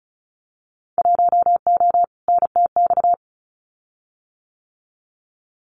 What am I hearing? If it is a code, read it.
Morse code, heard as a keyed tone, about 34 wpm, 700 Hz. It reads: 1O NTX
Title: Winter Field Day Exchanges 35wpm